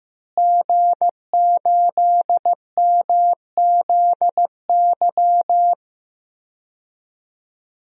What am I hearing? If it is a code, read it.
Morse code, heard as a keyed tone, about 15 wpm, 700 Hz. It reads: G8MZY
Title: DXCC-100 Call Signs - 5 Characters - Repeated 2x - 15wpm